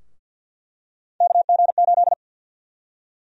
Morse code, heard as a keyed tone, about 50 wpm, 700 Hz. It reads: KG8
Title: USA Call Sign Prefixes - Repeated 2x - 50wpm